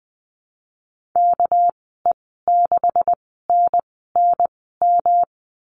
Morse code, heard as a keyed tone, about 20 wpm, spaced as for 15 wpm, 700 Hz. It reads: KE6NNM
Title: 200 Lesson 32g - Letter k - Call Signs 20wpm